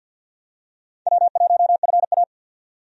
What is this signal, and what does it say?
Morse code, heard as a keyed tone, about 53 wpm, 700 Hz. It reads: W1FA
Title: USA Call Signs - 4 Characters - 50wpm